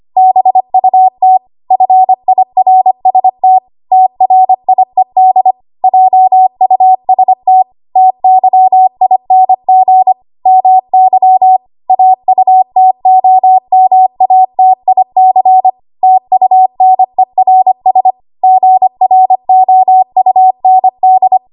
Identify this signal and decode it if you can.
Morse code, heard as a keyed tone, about 25 wpm, 750 Hz. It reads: BUT FIRST TRIED JUST TYING MY AUTOMATIC TUNERS GROUND
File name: Archive / 25 WPM / 25WPM